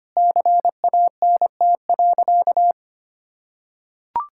Morse code, heard as a keyed tone, about 25 wpm, 700 Hz. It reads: CANT.
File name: practice / 25wpm / Sentences with 2-Word Phrases - 25wpm